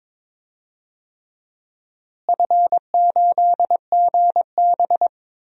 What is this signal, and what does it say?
Morse code, heard as a keyed tone, about 22 wpm, 700 Hz. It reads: F8GB
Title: DXCC-100 Call Signs - 4 Characters - Repeated 3x - 22wpm